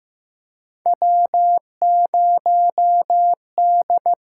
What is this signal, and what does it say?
Morse code, heard as a keyed tone, about 15 wpm, 700 Hz. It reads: W0D
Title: USA Call Signs - 4 Characters - 8x Character Spacing - 15wpm